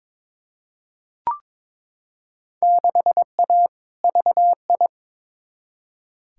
Morse code, heard as a keyed tone, about 22 wpm, 700 Hz. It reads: 6A VI
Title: Field Day Exchanges Encore - 22wpm